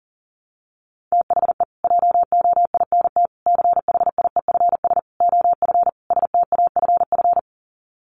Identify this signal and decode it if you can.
Morse code, heard as a keyed tone, about 40 wpm, 700 Hz. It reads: THE JOINT CHIEFS OF STAFF
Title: Sets of 5 Words Encore2 40wpm